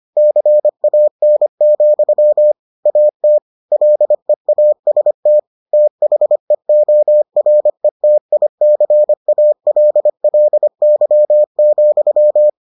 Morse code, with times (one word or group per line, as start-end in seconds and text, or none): 0.2-2.5 CAN,
2.9-3.4 AT
3.7-5.4 LEAST
5.7-12.5 THEORETICALLY,